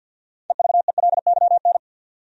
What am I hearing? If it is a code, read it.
Morse code, heard as a keyed tone, about 51 wpm, 700 Hz. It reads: EVELYN
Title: USA Common Names - 50wpm